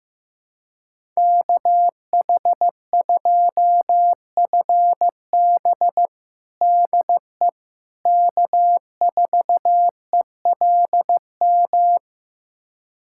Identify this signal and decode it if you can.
Morse code, heard as a keyed tone, about 15 wpm, 700 Hz. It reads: KH2FB DE K4ELM
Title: USA Callsign DE Callsign - 5 Characters - 15wpm